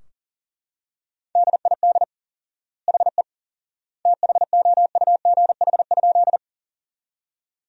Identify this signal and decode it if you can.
Morse code, heard as a keyed tone, about 40 wpm, 700 Hz. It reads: DID HE THOUGH?
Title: 4x Word-Spacing - Sentences from Top 400 Words - 40wpm